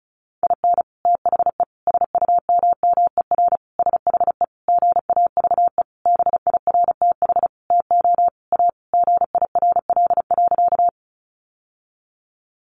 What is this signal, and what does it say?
Morse code, heard as a keyed tone, about 34 wpm, 700 Hz. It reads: IN THE SUMMER SHE GAVE BIRTH TO A GIRL.
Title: Sentences from Top 1100 Words 35wpm